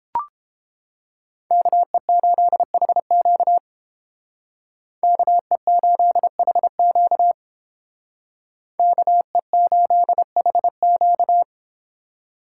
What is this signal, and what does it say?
Morse code, heard as a keyed tone, about 31 wpm, 700 Hz. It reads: KE8HQ KE8HQ KE8HQ